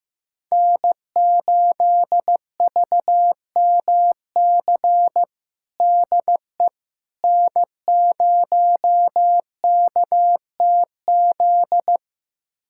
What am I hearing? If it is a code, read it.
Morse code, heard as a keyed tone, about 15 wpm, 700 Hz. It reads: N8VMC DE N0KTZ
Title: USA Callsign DE Callsign - 5 Characters - 15wpm